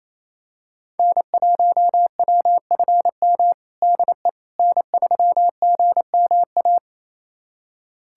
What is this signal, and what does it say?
Morse code, heard as a keyed tone, about 27 wpm, 700 Hz. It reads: N1WFM DE N3GMA
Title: USA Callsign DE Callsign - 5 Characters - 28wpm